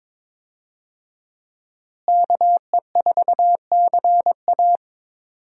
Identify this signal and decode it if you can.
Morse code, heard as a keyed tone, about 22 wpm, 700 Hz. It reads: KE4CA
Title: USA Call Signs - 5 Characters - Repeated 2x - 22wpm